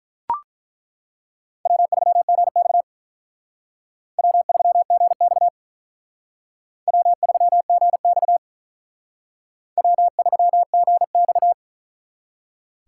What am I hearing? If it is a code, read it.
Morse code, heard as a keyed tone, about 43 wpm, 700 Hz. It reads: W3GX W3GX W3GX W3GX
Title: Speed Racing - USA 1x2 Call Signs 35wpm